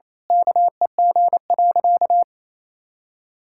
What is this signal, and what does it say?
Morse code, heard as a keyed tone, about 28 wpm, 700 Hz. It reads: KEG.
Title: Pangram Challenge v2 - 28wpm